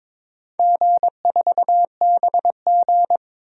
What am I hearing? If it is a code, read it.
Morse code, heard as a keyed tone, about 21 wpm, 700 Hz. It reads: G4BG
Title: DXCC-100 Call Signs - 4 Characters - 22wpm